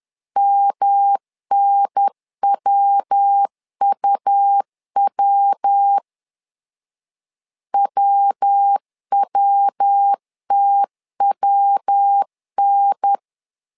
Morse code, heard as a keyed tone, about 10 wpm, 800 Hz. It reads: MNWUW WWTWN
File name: LEZIONI / 3 MWRO / 3 MW 10wpm 1959